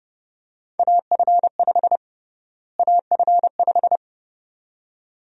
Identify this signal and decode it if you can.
Morse code, heard as a keyed tone, about 32 wpm, 700 Hz. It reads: AF5 AF5
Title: USA Call Sign Prefixes - Repeated 2x - 30wpm